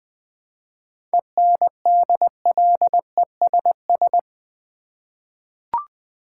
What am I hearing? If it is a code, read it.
Morse code, heard as a keyed tone, about 20 wpm, 700 Hz. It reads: ENDLESS
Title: 7 Letter Words - First 3 Letters Given - 20wpm